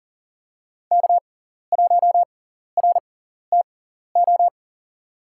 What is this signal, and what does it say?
Morse code, heard as a keyed tone, about 37 wpm, spaced as for 13 wpm, 700 Hz. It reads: K1RTO
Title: USA Call Signs - 5 Characters - 6x Character Spacing - 40wpm